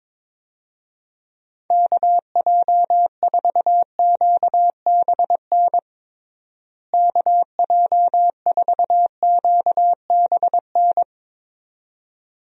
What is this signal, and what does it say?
Morse code, heard as a keyed tone, about 22 wpm, 700 Hz. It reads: KJ4QBN KJ4QBN
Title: USA Call Signs - 6 Characters - Repeated 2x - 22wpm